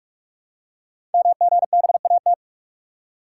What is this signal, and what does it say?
Morse code, heard as a keyed tone, about 44 wpm, 700 Hz. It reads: MGBAT
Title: Code Groups - 5 Letter:Numbers - 45wpm